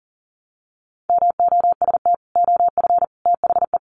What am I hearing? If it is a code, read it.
Morse code, heard as a keyed tone, about 40 wpm, 700 Hz. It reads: MOST OF THE
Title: Sets of 3 Words 40wpm